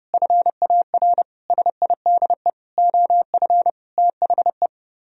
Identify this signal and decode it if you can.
Morse code, heard as a keyed tone, about 32 wpm, 700 Hz. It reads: FAR SIDE OF THE